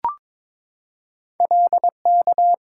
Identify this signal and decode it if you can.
Morse code, heard as a keyed tone, about 22 wpm, 700 Hz. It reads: LK
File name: practice / 22wpm / Two Letter-Number Combinations - 22wpm